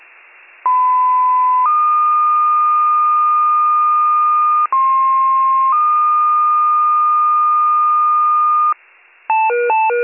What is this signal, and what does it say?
Two-tone page, heard as a one-second tone, then a three-second tone, about 1010 then 1220 Hz, about 1060 then 1220 Hz. It is heard from 0.7 to 8.7 s.